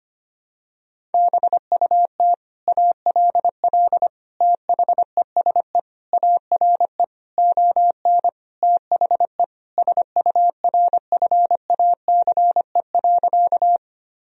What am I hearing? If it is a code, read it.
Morse code, heard as a keyed tone, about 25 wpm, 700 Hz. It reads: BUT ALL THESE ARE ON THE SURFACE.
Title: Sentences from Top 1000 Words - Review Part 3 - 25wpm